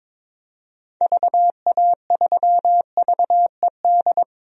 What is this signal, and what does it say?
Morse code, heard as a keyed tone, about 22 wpm, 700 Hz. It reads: VA3VED
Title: DXCC-100 Call Signs - 6 Characters - 22wpm